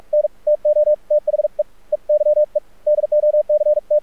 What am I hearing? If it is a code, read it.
Morse code, heard as a keyed tone, about 46 wpm, 600 Hz. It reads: N TO THE EYE DOC T
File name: archived / CRSnet